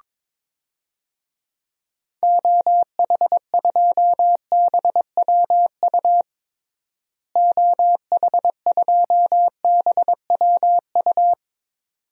Morse code, heard as a keyed tone, about 22 wpm, 700 Hz. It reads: OH2BWU OH2BWU
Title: DXCC-100 Call Signs - 6 Characters - Repeated 2x - 22wpm